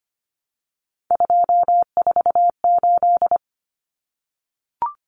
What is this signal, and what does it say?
Morse code, heard as a keyed tone, about 25 wpm, 700 Hz. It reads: 248
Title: Numbers - 3-Digit 25wpm